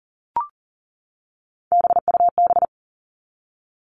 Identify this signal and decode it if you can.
Morse code, heard as a keyed tone, about 40 wpm, 700 Hz. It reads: BUB
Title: All Cut Numbers - 3-Digit 40wpm